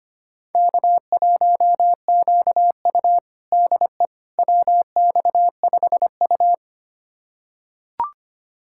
Morse code, heard as a keyed tone, about 25 wpm, 700 Hz. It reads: K1QU DE WX5U
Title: USA Callsign DE Callsign - 4 Characters - 25wpm